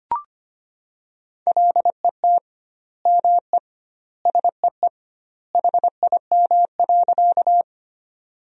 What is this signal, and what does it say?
Morse code, heard as a keyed tone, about 25 wpm, 700 Hz. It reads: LET ME SEE HIM.
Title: 2x Word-Spacing - Sentences from Top 300 Words - 25wpm